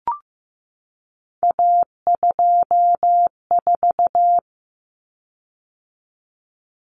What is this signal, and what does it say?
Morse code, heard as a keyed tone, about 15 wpm, 700 Hz. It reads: A24
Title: Common Cut Numbers - 3-Digit 15wpm